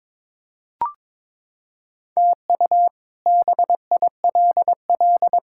Morse code, heard as a keyed tone, about 22 wpm, 700 Hz. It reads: TU BILL
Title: K1USN SST QSO Practice - 22wpm